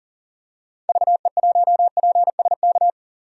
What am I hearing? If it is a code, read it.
Morse code, heard as a keyed tone, about 38 wpm, 700 Hz. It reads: VE1PSK
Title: DXCC-100 Call Signs - 6 Characters - 40wpm